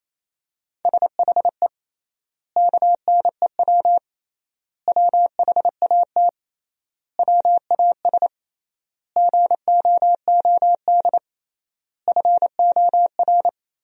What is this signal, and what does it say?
Morse code, heard as a keyed tone, about 28 wpm, 700 Hz. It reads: SHE KNEW WHAT WAS GOOD FOR
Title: 3x Word-Spacing - Sentences from Top 400 Words - 28wpm